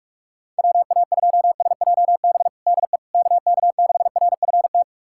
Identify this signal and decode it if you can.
Morse code, heard as a keyed tone, about 41 wpm, 700 Hz. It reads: WA2SJB DE KK6RFT
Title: USA Callsign DE Callsign - 6 Characters - 45wpm